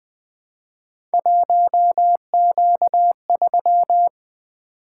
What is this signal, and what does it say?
Morse code, heard as a keyed tone, about 20 wpm, 700 Hz. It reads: 1Q3
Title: Three Letter-Number Combinations - 20wpm